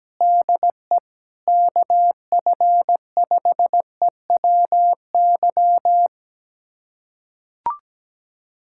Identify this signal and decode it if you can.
Morse code, heard as a keyed tone, about 17 wpm, 700 Hz. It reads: DE KF5EWY
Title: USA Callsign DE Callsign - 6 Characters - 17wpm